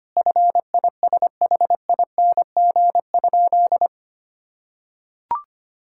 Morse code, heard as a keyed tone, about 25 wpm, 700 Hz. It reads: FISHING?
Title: Sentences from Top 1000 Words - Review Part 5 - 25wpm